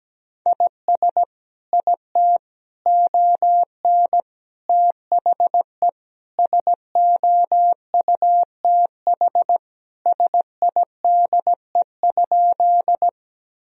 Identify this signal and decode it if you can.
Morse code, heard as a keyed tone, about 17 wpm, 700 Hz. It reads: IS IT ON THE SOUTH SIDE?
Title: Sentences from Top 1000 Words - Review Part 2 - 17wpm